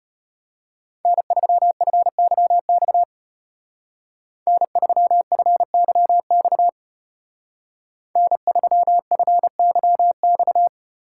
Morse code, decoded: N3FYX N3FYX N3FYX

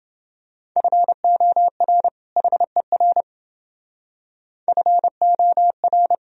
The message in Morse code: FOR HER FOR

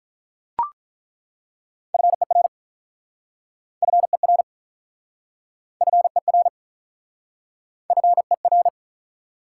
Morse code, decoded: FER FER FER FER

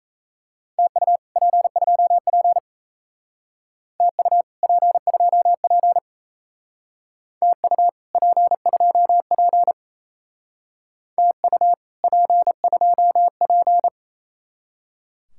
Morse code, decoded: TU P2P TU P2P TU P2P TU P2P

